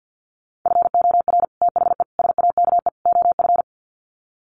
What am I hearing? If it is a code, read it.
Morse code, heard as a keyed tone, about 50 wpm, 700 Hz. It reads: FOR THE SAKE OF
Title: Sets of 4 Words 50wpm